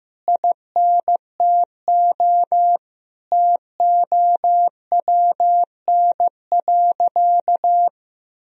Morse code, heard as a keyed tone, about 15 wpm, 700 Hz. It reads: INTO TOWN.
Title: Sentences from Top 1000 Words - Review Part 2 - 15wpm